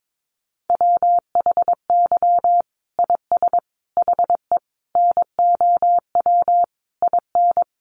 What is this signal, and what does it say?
Morse code, heard as a keyed tone, about 22 wpm, 700 Hz. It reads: WHY IS HE NOW IN
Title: Sentences from Top 1300 Words 22wpm